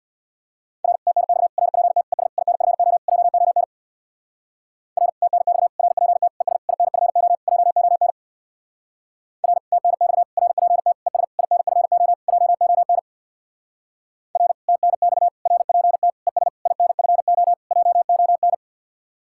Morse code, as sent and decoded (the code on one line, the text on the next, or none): .-. / - -. -..- / .-. .--. - / . ... / .. -. ..-. --- / .--- --- -. / .-. / - -. -..- / .-. .--. - / . ... / .. -. ..-. --- / .--- --- -. / .-. / - -. -..- / .-. .--. - / . ... / .. -. ..-. --- / .--- --- -. / .-. / - -. -..- / .-. .--. - / . ... / .. -. ..-. --- / .--- --- -.
R TNX RPT ES INFO JON R TNX RPT ES INFO JON R TNX RPT ES INFO JON R TNX RPT ES INFO JON